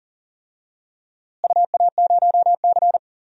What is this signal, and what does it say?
Morse code, heard as a keyed tone, about 40 wpm, 700 Hz. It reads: UA0C
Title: DXCC-100 Call Signs - 4 Characters - Repeated 2x - 40wpm